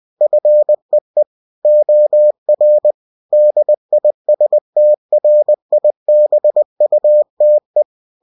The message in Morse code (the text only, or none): FEE OR DISTRIBUTE